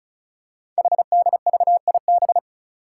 Morse code, heard as a keyed tone, about 36 wpm, 700 Hz. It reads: HDVIB
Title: Code Groups - 5 Letters - 35wpm